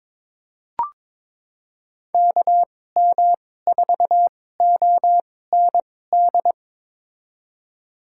Morse code, heard as a keyed tone, about 22 wpm, spaced as for 16 wpm, 700 Hz. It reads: KM4OND
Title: USA Call Signs - 6 Characters - 2x Character Spacing - 22wpm